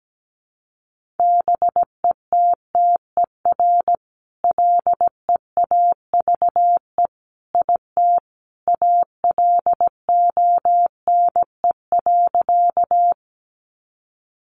Morse code, heard as a keyed tone, about 17 wpm, 700 Hz. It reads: BETTER LEAVE IT ALONE.
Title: Sentences from Top 1000 Words 17wpm